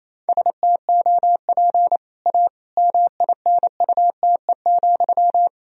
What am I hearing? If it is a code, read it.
Morse code, heard as a keyed tone, about 27 wpm, 700 Hz. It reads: STOP A MINUTE,
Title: Sentences from Top 1000 Words - Review Part 2 - 28wpm